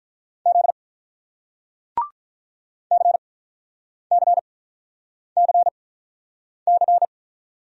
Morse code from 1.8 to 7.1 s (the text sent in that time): CCCC